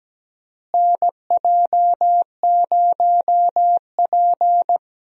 Morse code, heard as a keyed tone, about 17 wpm, 700 Hz. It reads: NJ0P
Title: USA Call Signs - 4 Characters - Repeated 3x - 17wpm